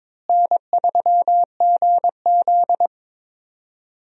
Morse code, heard as a keyed tone, about 22 wpm, 700 Hz. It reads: N3GZ
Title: USA Call Signs - 4 Characters - Repeated 2x - 22wpm